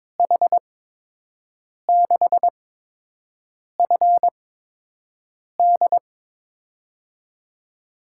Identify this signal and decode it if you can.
Morse code, heard as a keyed tone, about 22 wpm, spaced as for 6 wpm, 700 Hz. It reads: H6FD
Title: USA Call Signs - 8x Character Spacing - 22wpm